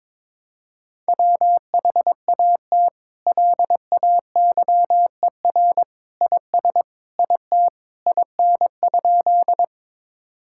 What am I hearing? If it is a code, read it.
Morse code, heard as a keyed tone, about 22 wpm, 700 Hz. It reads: WHAT LAYER IS IT IN?
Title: Sentences from Top 1600 Words 22wpm